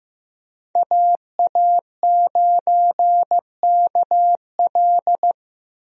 Morse code, heard as a keyed tone, about 15 wpm, 700 Hz. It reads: AA9KL
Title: USA Call Signs - 5 Characters - 15wpm